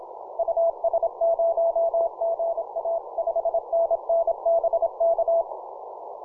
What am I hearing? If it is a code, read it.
Morse code, heard as a keyed tone, about 28 wpm, 670 Hz. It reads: US0GA 5NN BK
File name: simple-file-list / XF3 IZ2LSC 15mcw